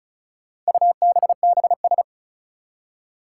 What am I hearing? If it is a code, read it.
Morse code, heard as a keyed tone, about 34 wpm, 700 Hz. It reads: UBBS
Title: Four Challenging Letter-Number Combinations 35wpm